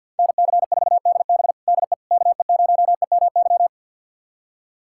Morse code, heard as a keyed tone, about 52 wpm, 700 Hz. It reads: NC4DB DE KE0EMY
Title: USA Callsign DE Callsign 50wpm